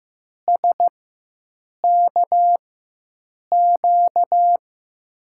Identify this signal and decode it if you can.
Morse code, heard as a keyed tone, about 15 wpm, spaced as for 7 wpm, 700 Hz. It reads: SKQ
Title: USA Call Signs - 6 Characters - 4x Character Spacing - 15wpm